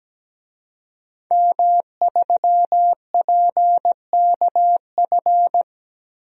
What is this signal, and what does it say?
Morse code, heard as a keyed tone, about 17 wpm, 700 Hz. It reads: M3PKF
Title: DXCC-100 Call Signs - 5 Characters - Repeated 3x - 17wpm